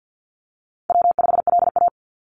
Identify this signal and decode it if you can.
Morse code, heard as a keyed tone, about 49 wpm, 700 Hz. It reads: W5LA
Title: Call Signs Repeated 2x - 50wpm